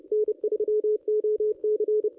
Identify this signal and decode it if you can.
Morse code, heard as a keyed tone, about 28 wpm, 440 Hz. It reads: N3OC
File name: contests / ss-cw-k5tr-op-n5rz -1200